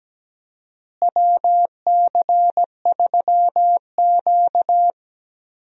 Morse code, heard as a keyed tone, about 17 wpm, 700 Hz. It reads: WC3Q
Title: USA Call Signs - 4 Characters - Repeated 2x - 17wpm